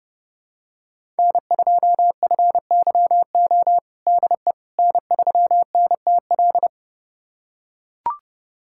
Morse code, heard as a keyed tone, about 32 wpm, 700 Hz. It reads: N2FYO DE N3NTL
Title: USA Callsign DE Callsign - 5 Characters - 30wpm